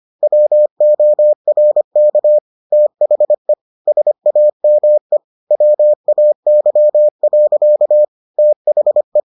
Morse code, heard as a keyed tone, about 25 wpm, 600 Hz. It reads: WORK THE SAME WAY. THE